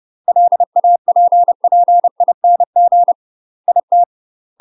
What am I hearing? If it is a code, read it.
Morse code, heard as a keyed tone, about 30 wpm, 700 Hz. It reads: LAPPING IT
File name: UK MASTER QSOs TRAINING / 30WPM / UK MASTER QSOs 0024